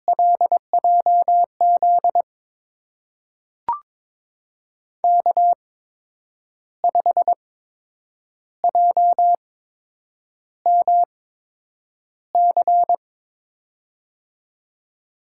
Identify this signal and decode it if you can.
Morse code, heard as a keyed tone, about 22 wpm, spaced as for 8 wpm, 700 Hz. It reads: LJZ K5JMC
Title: USA Call Signs - 5 Characters - 8x Character Spacing - 22wpm